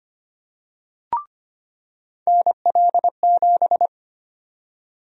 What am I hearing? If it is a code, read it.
Morse code, heard as a keyed tone, about 25 wpm, 700 Hz. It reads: NL7
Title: USA Call Sign Prefixes - 25wpm